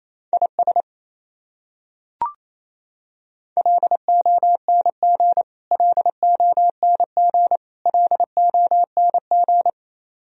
Morse code, heard as a keyed tone, about 28 wpm, 700 Hz. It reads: IS LONG LONG LONG
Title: Top 100 Words Repeated 28wpm